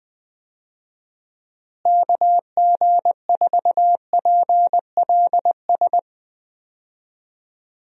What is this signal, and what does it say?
Morse code, heard as a keyed tone, about 20 wpm, 700 Hz. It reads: KG4PLS
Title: USA Call Signs - 6 Characters - Repeated 3x - 20wpm